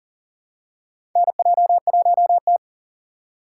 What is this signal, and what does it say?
Morse code, heard as a keyed tone, about 40 wpm, 700 Hz. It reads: NJ1T
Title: K1USN SST QSO Practice - 40wpm